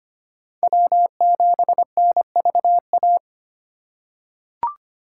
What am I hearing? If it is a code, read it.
Morse code, heard as a keyed tone, about 25 wpm, 700 Hz. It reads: W7NVA
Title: USA Call Signs - 5 Characters - 4x Character Spacing - 25wpm